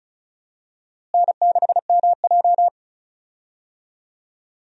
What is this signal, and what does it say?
Morse code, heard as a keyed tone, about 36 wpm, 700 Hz. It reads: N6MJ
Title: USA Call Signs - 4 Characters - Repeated 3x - 35wpm